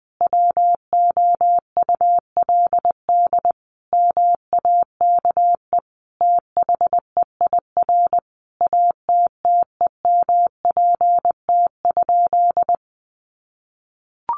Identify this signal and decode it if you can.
Morse code, heard as a keyed tone, about 19 wpm, 700 Hz. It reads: WOULD MAKE THEIR ATTEMPT?
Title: Sentences from Top 1000 Words 20wpm